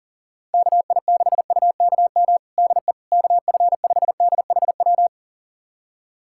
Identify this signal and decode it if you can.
Morse code, heard as a keyed tone, about 40 wpm, 700 Hz. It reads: KI6UKM DE KF5DHW
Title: USA Callsign DE Callsign - 6 Characters - 40wpm